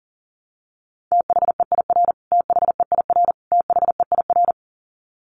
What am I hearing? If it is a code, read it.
Morse code, heard as a keyed tone, about 40 wpm, 700 Hz. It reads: THEIR THEIR THEIR
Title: Top 300 Words Repeated 40wpm